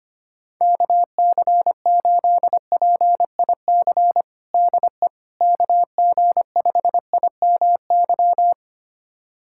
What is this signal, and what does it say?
Morse code, heard as a keyed tone, about 25 wpm, 700 Hz. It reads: KC8PIC DE KG5IMY
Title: USA Callsign DE Callsign - 6 Characters - 25wpm